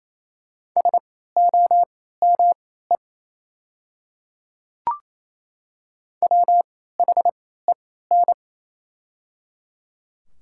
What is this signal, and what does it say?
Morse code, heard as a keyed tone, about 28 wpm, spaced as for 16 wpm, 700 Hz. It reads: SOME WHEN
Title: 3x Character Spacing - Top 200 Words - 28wpm